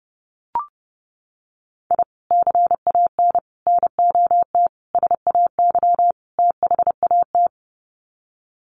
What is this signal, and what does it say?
Morse code, heard as a keyed tone, about 30 wpm, 700 Hz. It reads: I CAN NOT SAY THAT
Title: Sets of 5 Words Encore2 30wpm